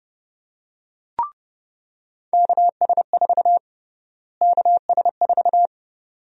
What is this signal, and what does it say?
Morse code, heard as a keyed tone, about 30 wpm, 700 Hz. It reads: KS4 KS4
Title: USA Call Sign Prefixes - Repeated 2x - 30wpm